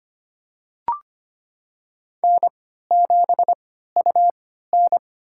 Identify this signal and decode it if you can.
Morse code, heard as a keyed tone, about 25 wpm, spaced as for 14 wpm, 700 Hz. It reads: N7UN
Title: USA Call Signs - 3x Character Spacing - 25wpm